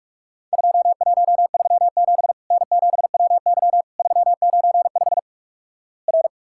Morse code, heard as a keyed tone, about 46 wpm, 700 Hz. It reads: 2137 N7WY 395 R
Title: Worked All Europe DX Contest - QTC Exchange - 45wpm